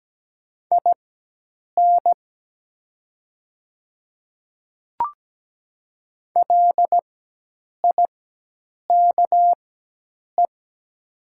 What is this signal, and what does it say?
Morse code, heard as a keyed tone, about 17 wpm, spaced as for 8 wpm, 700 Hz. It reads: IN LIKE
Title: 4x Character Spacing - Top 100 Words - 17wpm